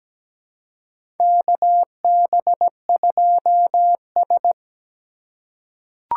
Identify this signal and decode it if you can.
Morse code, heard as a keyed tone, about 17 wpm, 700 Hz. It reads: KB2S
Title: K1USN SST QSO Practice - 17wpm